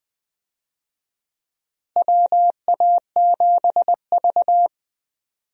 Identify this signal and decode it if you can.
Morse code, heard as a keyed tone, about 20 wpm, 700 Hz. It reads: WA7V
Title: USA Call Signs - 4 Characters - Repeated 2x - 20wpm